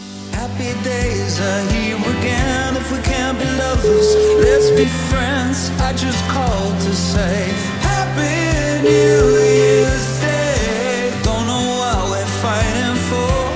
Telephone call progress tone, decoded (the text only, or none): ringback tone